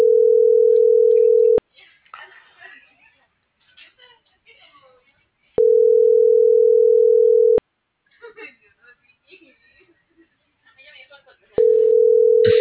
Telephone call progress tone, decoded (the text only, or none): ringback tone